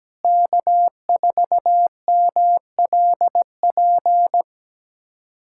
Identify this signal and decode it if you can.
Morse code, heard as a keyed tone, about 17 wpm, 700 Hz. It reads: K4MLP
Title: USA Callsign DE Callsign - 5 Characters - 17wpm